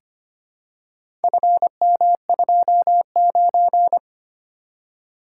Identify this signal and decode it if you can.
Morse code, heard as a keyed tone, about 25 wpm, 700 Hz. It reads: FM29
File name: practice / 25wpm / Four Letter-Number Combinations - 25wpm